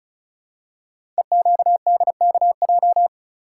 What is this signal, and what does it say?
Morse code, heard as a keyed tone, about 36 wpm, 690 Hz. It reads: EQDKJ
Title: Code Groups - 5 Letters - 35wpm